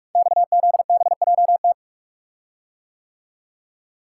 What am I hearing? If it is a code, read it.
Morse code, heard as a keyed tone, about 45 wpm, 700 Hz. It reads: XZBJT
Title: Code Groups - 5 Letter:Numbers - 45wpm